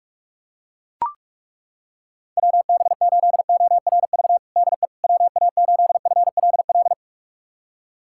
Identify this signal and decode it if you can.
Morse code, heard as a keyed tone, about 41 wpm, 700 Hz. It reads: WB8ORV DE WA8FLL
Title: USA Callsign DE Callsign - 6 Characters - 45wpm